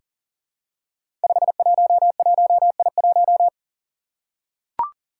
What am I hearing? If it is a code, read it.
Morse code, heard as a keyed tone, about 39 wpm, 700 Hz. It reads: 511I1
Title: Five Challenging Letter-Number Combinations 40wpm